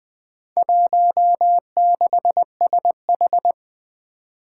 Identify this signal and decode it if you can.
Morse code, heard as a keyed tone, about 20 wpm, 700 Hz. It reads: 16SH
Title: Four Challenging Letter-Number Combinations 20wpm